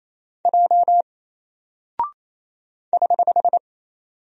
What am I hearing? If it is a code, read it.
Morse code, heard as a keyed tone, about 28 wpm, 700 Hz. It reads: J <HH>
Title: Single Character (Letter:Number:Punctuation:Prosign) Alt 28wpm